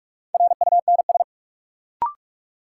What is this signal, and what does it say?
Morse code, heard as a keyed tone, about 46 wpm, 700 Hz. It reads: RUNS
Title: Sentences from Top 600 Words - New Word Drill - 45wpm